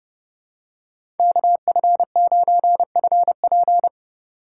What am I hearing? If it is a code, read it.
Morse code, heard as a keyed tone, about 30 wpm, 700 Hz. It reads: KF9FP
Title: USA Call Signs - 4x Character Spacing - 30wpm